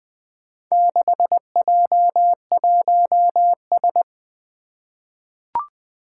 Morse code, heard as a keyed tone, about 20 wpm, 700 Hz. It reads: 6J1S
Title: Four Challenging Letter-Number Combinations 20wpm